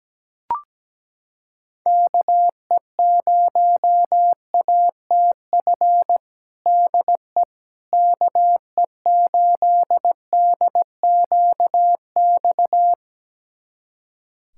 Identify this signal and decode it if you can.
Morse code, heard as a keyed tone, about 17 wpm, 700 Hz. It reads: KE0ATF DE KE8DQX
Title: USA Callsign DE Callsign 17wpm